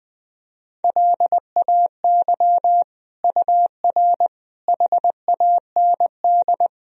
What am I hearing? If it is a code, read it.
Morse code, heard as a keyed tone, about 20 wpm, 700 Hz. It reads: LAY UR HAND
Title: Sentences from Top 400 Words and Abbreviations 20wpm